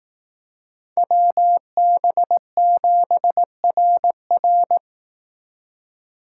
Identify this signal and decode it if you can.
Morse code, heard as a keyed tone, about 18 wpm, 700 Hz. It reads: WB7RR
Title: Speed Racing - USA 2x2 Call Signs 15wpm